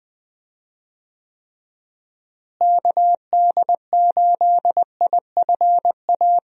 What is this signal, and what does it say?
Morse code, heard as a keyed tone, about 20 wpm, 700 Hz. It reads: KD8IFA
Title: USA Call Signs - 6 Characters - Repeated 3x - 20wpm